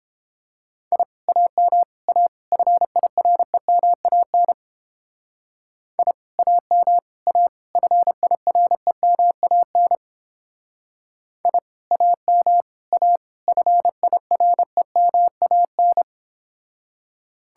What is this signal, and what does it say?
Morse code, heard as a keyed tone, about 30 wpm, 700 Hz. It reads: I AM A FIREMAN I AM A FIREMAN I AM A FIREMAN